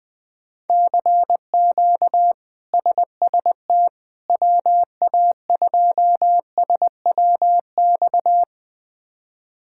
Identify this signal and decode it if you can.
Morse code, heard as a keyed tone, about 20 wpm, 700 Hz. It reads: CQ SST WA2SWX
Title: K1USN SST QSO Practice - 20wpm